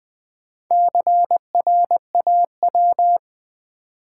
Morse code, heard as a keyed tone, about 20 wpm, 700 Hz. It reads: CRAW
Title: Build A Word - 1 Letter At A Time - 5-Letter Words - 20wpm